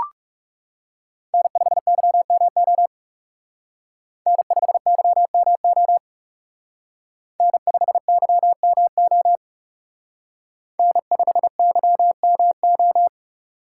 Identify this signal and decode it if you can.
Morse code, heard as a keyed tone, about 36 wpm, 700 Hz. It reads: N5YMO N5YMO N5YMO N5YMO
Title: Speed Racing - USA 1x3 Call Signs 30wpm